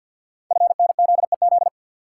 Morse code, heard as a keyed tone, about 50 wpm, 690 Hz. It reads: FNZEZ
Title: Code Groups - 5 Letters - 50wpm